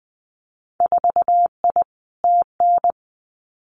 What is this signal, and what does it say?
Morse code, heard as a keyed tone, about 20 wpm, 700 Hz. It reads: 4I TN
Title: Winter Field Day Exchanges 20wpm